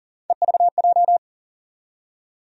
Morse code, heard as a keyed tone, about 40 wpm, 700 Hz. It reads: EVJ
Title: USA Call Sign Postfixes - 3 Letters 40wpm